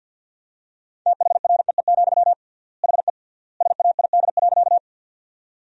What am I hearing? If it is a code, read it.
Morse code, heard as a keyed tone, about 50 wpm, 700 Hz. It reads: THREE, HE SAID.
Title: 3x Word-Spacing - Sentences from Top 200 Words - 50wpm